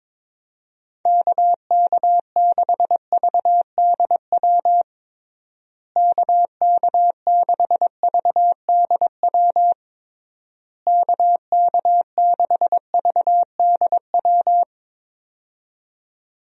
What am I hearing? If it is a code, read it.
Morse code, heard as a keyed tone, about 22 wpm, 700 Hz. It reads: KK6VDW KK6VDW KK6VDW
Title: USA Call Signs - 6 Characters - Repeated 3x - 22wpm